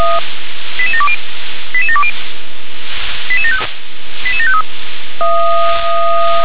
Five-tone selective call (ZVEI), heard as quick runs of five tones, tones of about 70 ms. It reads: 88720 88720 88753